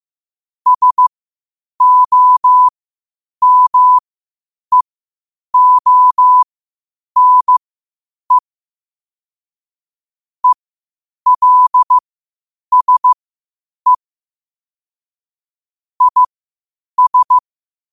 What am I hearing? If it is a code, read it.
Morse code, heard as a keyed tone, about 15 wpm, spaced as for 8 wpm, 1000 Hz. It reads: SOMEONE ELSE IS